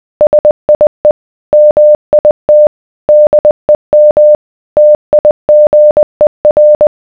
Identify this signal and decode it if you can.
Morse code, heard as a keyed tone, about 20 wpm, 600 Hz. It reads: SIE MIT DEM TIGER